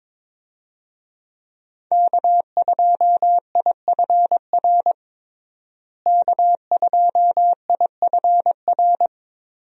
Morse code, heard as a keyed tone, about 21 wpm, 700 Hz. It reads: K2IFR K2IFR
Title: USA Call Signs - 5 Characters - Repeated 3x - 22wpm